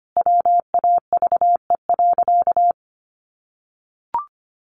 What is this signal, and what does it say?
Morse code, heard as a keyed tone, about 25 wpm, 700 Hz. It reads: WAVE.
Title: Sentences from Top 1400 Words 25wpm